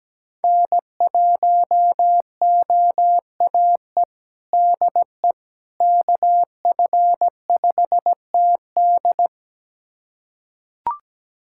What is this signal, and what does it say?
Morse code, heard as a keyed tone, about 17 wpm, 700 Hz. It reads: N1OAE DE KF5TD
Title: USA Callsign DE Callsign - 5 Characters - 17wpm